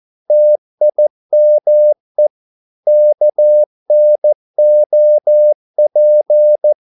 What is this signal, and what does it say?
Morse code, heard as a keyed tone, about 14 wpm, 600 Hz. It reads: TIME KNOP
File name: Korte ord DK 14wpm